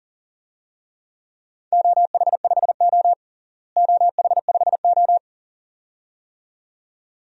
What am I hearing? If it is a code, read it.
Morse code, heard as a keyed tone, about 40 wpm, 700 Hz. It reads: OH5O OH5O
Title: DXCC-100 Call Signs - 4 Characters - Repeated 2x - 40wpm